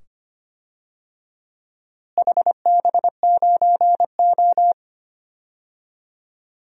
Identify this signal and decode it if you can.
Morse code, heard as a keyed tone, about 25 wpm, 700 Hz. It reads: HB9O
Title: DXCC-100 Call Signs - 4 Characters - Repeated 2x - 25wpm